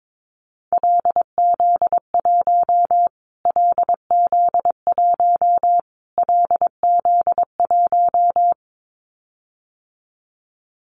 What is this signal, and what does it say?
Morse code, heard as a keyed tone, about 22 wpm, 700 Hz. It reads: LZ1 LZ1 LZ1
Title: Callsign Prefixes for Top 25 DXCC - Repeated 3x - 22wpm